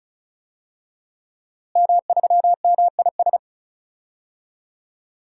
Morse code, heard as a keyed tone, about 36 wpm, 700 Hz. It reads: M3MIS
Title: DXCC-100 Call Signs - 5 Characters - Repeated 2x - 35wpm